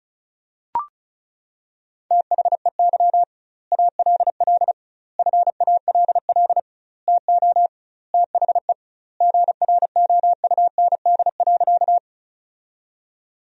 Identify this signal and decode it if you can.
Morse code, heard as a keyed tone, about 34 wpm, 700 Hz. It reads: THEY ALL FALL TO THE GROUND.
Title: 2x Word-Spacing - Sentences from Top 500 Words - 35wpm